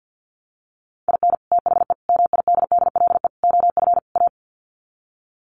Morse code, heard as a keyed tone, about 49 wpm, 700 Hz. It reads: IN THE MIDDLE OF A
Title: Sets of 5 Words 50wpm